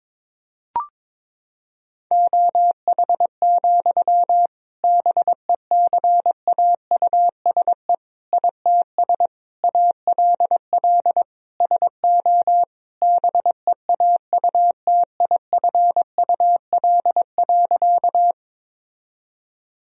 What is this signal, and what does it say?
Morse code, heard as a keyed tone, about 22 wpm, 700 Hz. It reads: OH, BECAUSE ITS ALL SO BEAUTIFUL.
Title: Sentences from Top 1000 Words - Review Part 3 - 22wpm